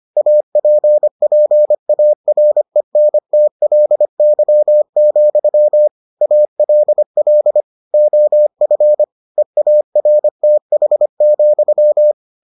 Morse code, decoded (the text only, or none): APPARENTLY, ALL OF EARTH,